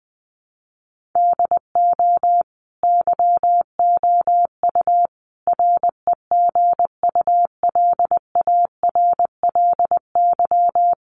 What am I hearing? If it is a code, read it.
Morse code, heard as a keyed tone, about 20 wpm, 700 Hz. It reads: DO YOU REGULARLY
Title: Sentences from Top 1400 Words 20wpm